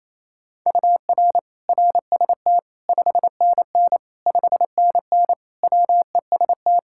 Morse code, heard as a keyed tone, about 28 wpm, 700 Hz. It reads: UR RST 5NN 5NN WEST
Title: K1USN SST QSO Practice 28wpm